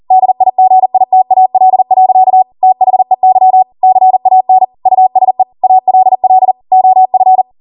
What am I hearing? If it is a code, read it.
Morse code, heard as a keyed tone, about 40 wpm, 750 Hz. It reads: DIGITAL. THEY CAN USE ALL OF